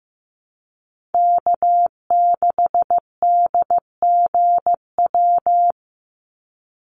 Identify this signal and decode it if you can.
Morse code, heard as a keyed tone, about 15 wpm, 700 Hz. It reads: K6DGW
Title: Call Signs Repeated 3x - 15wpm